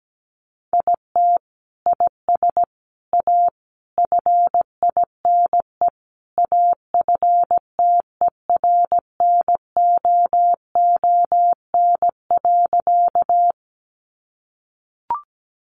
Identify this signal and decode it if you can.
Morse code, heard as a keyed tone, about 17 wpm, 700 Hz. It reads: IT IS A FINE AFTERNOON.
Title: Sentences from Top 1500 Words 17wpm